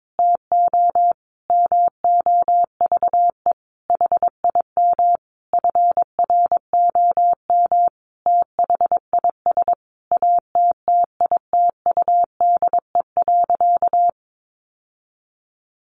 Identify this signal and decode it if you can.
Morse code, heard as a keyed tone, about 22 wpm, 700 Hz. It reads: TO MOVE HIM FROM THIS ATTITUDE.
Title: Sentences from Top 1300 Words 22wpm